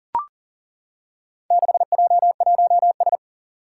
Morse code, heard as a keyed tone, about 40 wpm, 700 Hz. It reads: 6J1S
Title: Four Challenging Letter-Number Combinations 40wpm